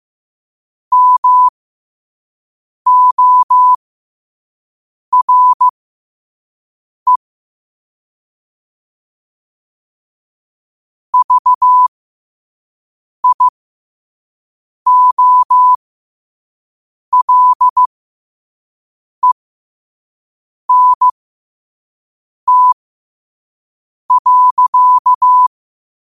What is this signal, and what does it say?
Morse code, heard as a keyed tone, about 15 wpm, spaced as for 5 wpm, 1000 Hz. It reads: MORE VIOLENT.
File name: qotdfiles / 5wpm